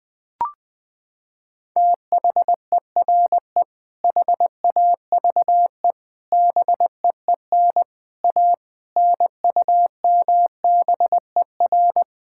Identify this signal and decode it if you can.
Morse code, heard as a keyed tone, about 20 wpm, 700 Hz. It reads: THERE HAVE BEEN A NUMBER
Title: Sets of 6 Words - 20wpm